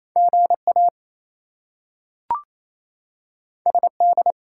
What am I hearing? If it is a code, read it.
Morse code, heard as a keyed tone, about 28 wpm, 700 Hz. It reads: GA SD
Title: US States and Canadian Provinces 28wpm